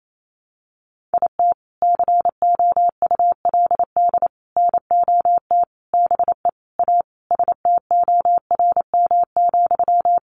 Morse code, heard as a keyed tone, about 28 wpm, 700 Hz. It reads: IT COULD NOT BE A STORM,
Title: Sentences from Top 1500 Words 28wpm